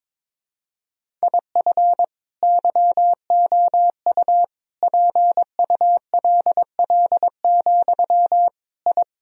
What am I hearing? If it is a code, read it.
Morse code, heard as a keyed tone, about 22 wpm, 700 Hz. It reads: IF YOU PULL, I
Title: Sentences from Top 1000 Words - Review Part 2 - 22wpm